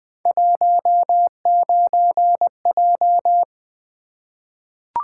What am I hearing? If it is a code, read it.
Morse code, heard as a keyed tone, about 20 wpm, 700 Hz. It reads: 19J
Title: Three Letter-Number Combinations - 20wpm